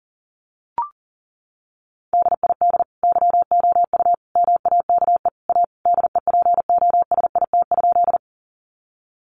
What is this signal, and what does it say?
Morse code, heard as a keyed tone, about 40 wpm, 700 Hz. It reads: DID YOU MAKE A DEPOSIT?
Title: Sentences from Top 1300 Words 40wpm